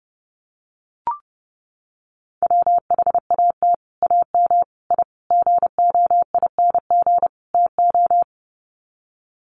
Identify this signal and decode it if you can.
Morse code, heard as a keyed tone, about 30 wpm, 700 Hz. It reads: WHAT AM I GOING TO
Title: Sets of 5 Words Encore 30wpm